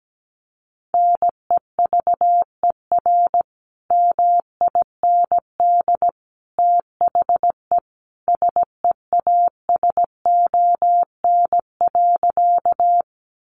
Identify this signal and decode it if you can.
Morse code, heard as a keyed tone, about 17 wpm, 700 Hz. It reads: NEVER MIND THE SEASON.
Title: Sentences from Top 1400 Words 17wpm